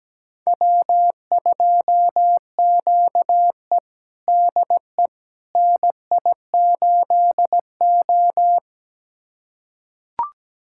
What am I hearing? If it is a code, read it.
Morse code, heard as a keyed tone, about 17 wpm, 700 Hz. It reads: W2QE DE NI8O
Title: USA Callsign DE Callsign - 4 Characters - 17wpm